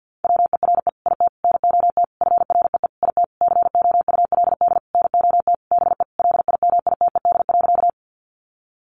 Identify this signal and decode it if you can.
Morse code, heard as a keyed tone, about 50 wpm, 700 Hz. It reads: WERE IT NOT FREE IT COULD NOT BE LIMITED.
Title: Sentences from Top 900 Words 50wpm